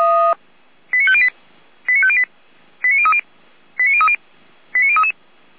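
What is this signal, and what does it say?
Five-tone selective call (ZVEI), heard as quick runs of five tones, tones of about 70 ms. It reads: 89508 89508 89030 89030 89033
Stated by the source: ZVEI five-tone selective call 89042, received by SDR.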